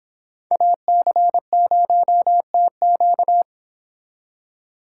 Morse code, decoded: AC0TQ

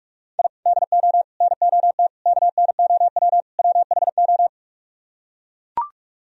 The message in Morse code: I DO NOT KNOW WHO

